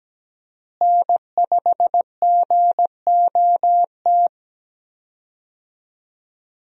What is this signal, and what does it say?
Morse code, heard as a keyed tone, about 17 wpm, 700 Hz. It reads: N5GOT
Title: USA Call Signs - 5 Characters - Repeated 2x - 17wpm